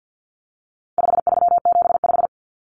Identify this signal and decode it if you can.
Morse code, heard as a keyed tone, about 50 wpm, 700 Hz. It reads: 5375
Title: Common Cut Numbers - 4-Digit 50wpm